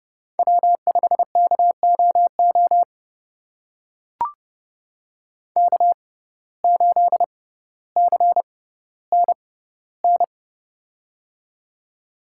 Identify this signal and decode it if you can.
Morse code, heard as a keyed tone, about 30 wpm, spaced as for 15 wpm, 700 Hz. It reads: W5KOO K8CNN
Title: USA Call Signs - 6x Character Spacing - 30wpm